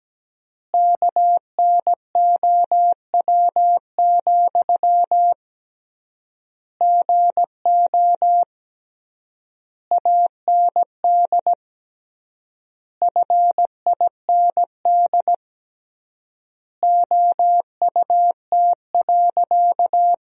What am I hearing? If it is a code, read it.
Morse code, heard as a keyed tone, about 17 wpm, 700 Hz. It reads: KNOW, GO AND FIND OUT.
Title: 3x Word-Spacing - Sentences from Top 200 Words - 17wpm